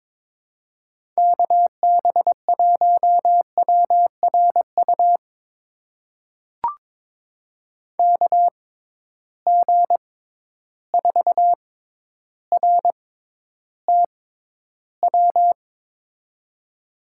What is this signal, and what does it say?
Morse code, heard as a keyed tone, about 21 wpm, spaced as for 11 wpm, 700 Hz. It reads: KB1WRU KG4RTW
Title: USA Call Signs - 6 Characters - 6x Character Spacing - 22wpm